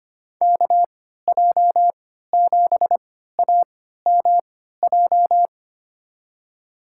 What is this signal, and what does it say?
Morse code, heard as a keyed tone, about 25 wpm, spaced as for 14 wpm, 700 Hz. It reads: KJ7AMJ
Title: USA Call Signs - 3x Character Spacing - 25wpm